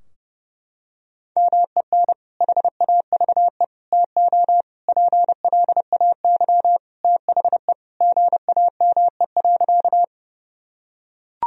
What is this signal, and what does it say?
Morse code, heard as a keyed tone, about 31 wpm, 700 Hz. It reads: MEN HAVE TO PLAY THE GAME.
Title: Sentences from Top 1000 Words - Review Part 3 - 30wpm